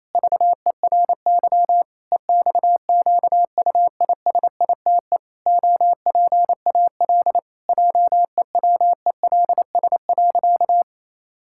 Morse code, heard as a keyed tone, about 28 wpm, 700 Hz. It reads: VERY EXQUISITE OPAL JEWELS.
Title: Pangram Challenge v2 - 28wpm